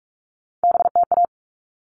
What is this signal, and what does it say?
Morse code, heard as a keyed tone, about 44 wpm, 700 Hz. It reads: BTA
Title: All Cut Numbers - 3-Digit 45wpm